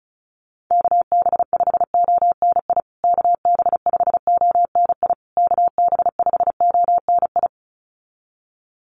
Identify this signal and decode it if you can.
Morse code, heard as a keyed tone, about 34 wpm, 700 Hz. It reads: KB5ONI KB5ONI KB5ONI
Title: USA Call Signs Repeated 3x 35wpm